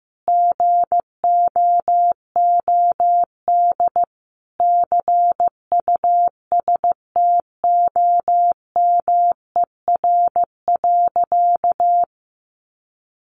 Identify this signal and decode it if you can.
Morse code, heard as a keyed tone, about 14 wpm, 700 Hz. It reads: GOOD CUSTOMER.
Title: Sentences from Top 1100 Words 15wpm